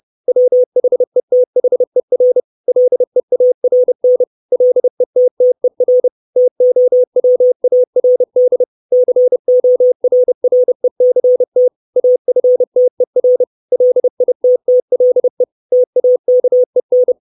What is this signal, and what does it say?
Morse code, heard as a keyed tone, about 30 wpm, 500 Hz. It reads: WHETHER LEARN LETTER TOWARD CORRECT AFTER LITTLE TAKEN
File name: 632 Words 30WPM